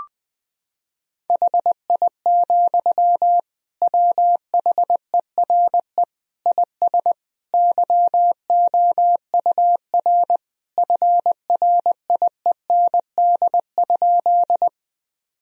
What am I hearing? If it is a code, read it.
Morse code, heard as a keyed tone, about 20 wpm, 700 Hz. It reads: HI, WHERE IS YOUR FRIEND?
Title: Sentences from Top 1000 Words - Review Part 3 - 20wpm